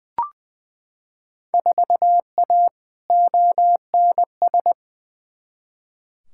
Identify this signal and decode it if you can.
Morse code, heard as a keyed tone, about 20 wpm, 700 Hz. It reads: 4A ONS
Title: Field Day Exchanges - 20wpm